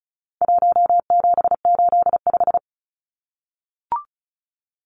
Morse code, heard as a keyed tone, about 34 wpm, 700 Hz. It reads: 1785
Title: Numbers - 4-Digit 35wpm